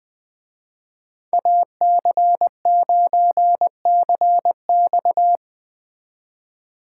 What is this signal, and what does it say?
Morse code, heard as a keyed tone, about 20 wpm, 700 Hz. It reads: AC9CX
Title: Speed Racing - USA 2x2 Call Signs 17wpm